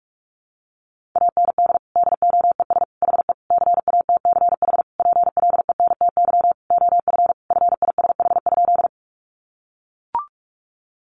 Morse code, heard as a keyed tone, about 44 wpm, 700 Hz. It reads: AND DOES HE CATCH PLENTY OF FISH?
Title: Sentences from Top 1500 Words 45wpm